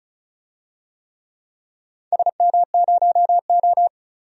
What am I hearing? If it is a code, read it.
Morse code, heard as a keyed tone, about 34 wpm, 700 Hz. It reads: SM0O
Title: DXCC-100 Call Signs - 4 Characters - 35wpm